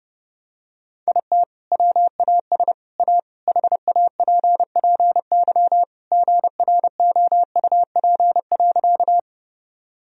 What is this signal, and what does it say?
Morse code, heard as a keyed tone, about 30 wpm, 700 Hz. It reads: IT WAS A HAPPY GROUP.
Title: Sentences from Top 1000 Words - Review Part 1 - 30wpm